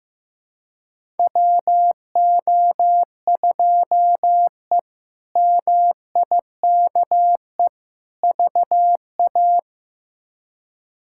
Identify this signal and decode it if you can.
Morse code, heard as a keyed tone, about 15 wpm, 700 Hz. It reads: WO2E MIKE VA
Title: K1USN SST QSO Practice - 15wpm